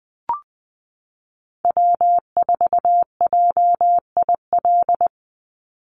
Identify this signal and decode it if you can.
Morse code, heard as a keyed tone, about 20 wpm, 700 Hz. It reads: W4JIL
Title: USA Call Signs 20wpm